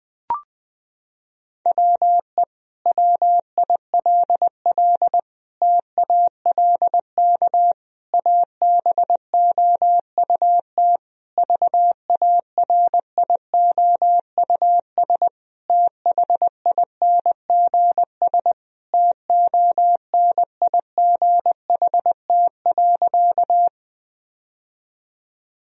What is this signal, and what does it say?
Morse code, heard as a keyed tone, about 20 wpm, 700 Hz. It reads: WE WILL TALK ABOUT VARIOUS THINGS TONIGHT.
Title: Sentences from Top 1000 Words - Review Part 4 - 20wpm